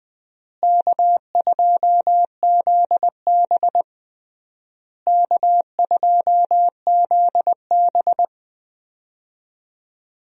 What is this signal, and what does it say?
Morse code, heard as a keyed tone, about 21 wpm, 700 Hz. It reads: K2ZB K2ZB
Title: USA Call Signs - 4 Characters - Repeated 3x - 20wpm